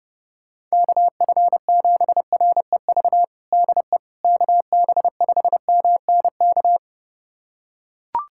Morse code, heard as a keyed tone, about 30 wpm, 700 Hz. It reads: KF7REV DE KB5MNK
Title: USA Callsign DE Callsign - 6 Characters - 30wpm